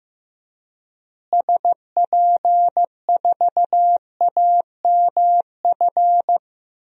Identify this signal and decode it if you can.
Morse code, heard as a keyed tone, about 15 wpm, 700 Hz. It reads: SP4AMF
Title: DXCC-100 Call Signs - 6 Characters - Repeated 2x - 15wpm